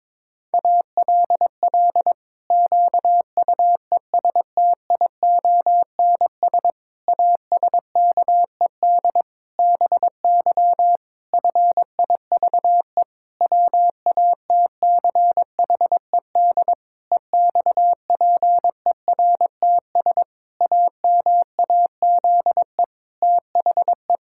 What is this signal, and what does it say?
Morse code, heard as a keyed tone, about 22 wpm, 700 Hz. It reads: ALL QUESTIONS ASKED BY FIVE WATCHED EXPERTS AMAZE THE
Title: Pangram Challenge v2 - 22wpm